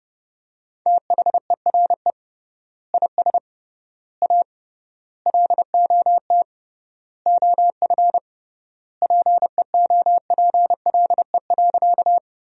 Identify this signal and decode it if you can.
Morse code, heard as a keyed tone, about 30 wpm, 700 Hz. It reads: THERE IS A LOT OF PEOPLE.
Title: 3x Word-Spacing - Sentences from Top 500 Words - 30wpm